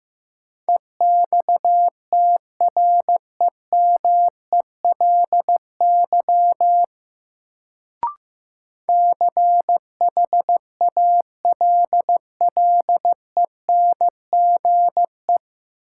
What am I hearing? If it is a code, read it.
Morse code, heard as a keyed tone, about 15 wpm, 700 Hz. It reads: EXTREMELY CHALLENGE
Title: Sentences from Top 1000 Words - Review Part 7 - New Word Drill - 15wpm